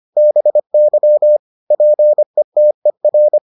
Morse code, heard as a keyed tone, about 25 wpm, 600 Hz. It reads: BY PETER